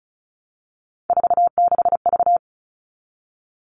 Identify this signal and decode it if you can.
Morse code, heard as a keyed tone, about 34 wpm, 700 Hz. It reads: 46V
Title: All Cut Numbers - 3-Digit 35wpm